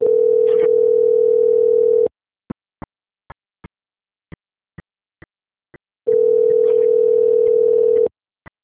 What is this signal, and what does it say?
Ringback tone (North America), 440 + 480 Hz.